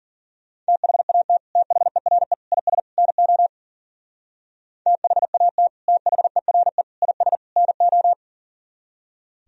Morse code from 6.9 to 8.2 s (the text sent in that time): IS NO